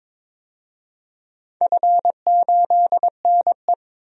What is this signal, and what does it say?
Morse code, heard as a keyed tone, about 22 wpm, 700 Hz. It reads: F8NE
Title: DXCC-100 Call Signs - 4 Characters - 22wpm